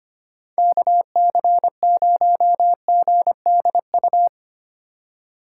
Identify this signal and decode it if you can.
Morse code, heard as a keyed tone, about 32 wpm, 700 Hz. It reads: KC0GDU